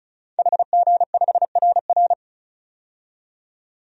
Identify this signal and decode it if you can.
Morse code, heard as a keyed tone, about 36 wpm, 700 Hz. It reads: HG5RR
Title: DXCC-100 Call Signs - 5 Characters - 35wpm